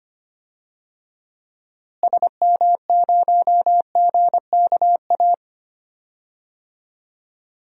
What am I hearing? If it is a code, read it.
Morse code, heard as a keyed tone, about 25 wpm, 700 Hz. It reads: SM0GKA
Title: DXCC-100 Call Signs - 6 Characters - Repeated 3x - 25wpm